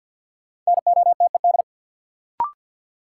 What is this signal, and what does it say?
Morse code, heard as a keyed tone, about 54 wpm, 700 Hz. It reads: NOTED
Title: Sets of 5 Words - New Word Drill - 50wpm